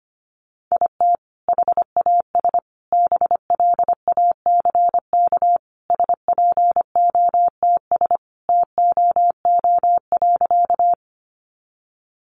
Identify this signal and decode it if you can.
Morse code, heard as a keyed tone, about 25 wpm, 700 Hz. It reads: IT HAS BLACK SPOTS TOO.
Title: Additional Easy Phrases 25wpm